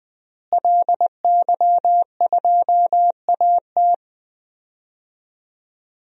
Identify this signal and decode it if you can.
Morse code, heard as a keyed tone, about 20 wpm, 700 Hz. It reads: LY2AT
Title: DXCC-100 Call Signs - 5 Characters - Repeated 2x - 20wpm